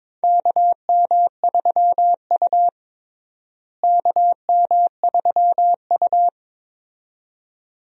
Morse code, heard as a keyed tone, about 22 wpm, 700 Hz. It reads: KM3U KM3U
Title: USA Call Signs - 4 Characters - Repeated 2x - 22wpm